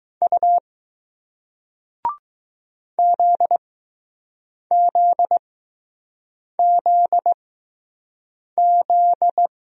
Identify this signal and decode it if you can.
Morse code, heard as a keyed tone, about 20 wpm, spaced as for 6 wpm, 700 Hz. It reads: U ZZZZ